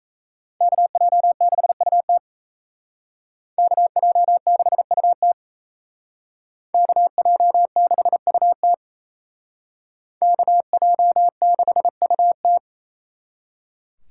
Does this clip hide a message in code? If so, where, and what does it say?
Yes: Morse code from 0.6 to 12.6 s, heard as a keyed tone, about 36 wpm, 700 Hz. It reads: KJ6UT KJ6UT KJ6UT KJ6UT